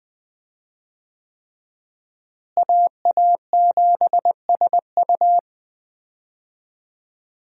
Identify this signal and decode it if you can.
Morse code, heard as a keyed tone, about 20 wpm, 700 Hz. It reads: AA7SU
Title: USA Call Signs - 5 Characters - Repeated 3x - 20wpm